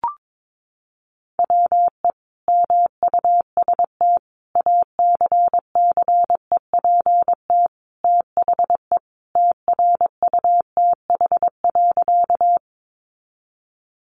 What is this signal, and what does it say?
Morse code, heard as a keyed tone, about 22 wpm, 700 Hz. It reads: WE MUST ACCEPT THE TRUTH.
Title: Sentences from Top 1300 Words 22wpm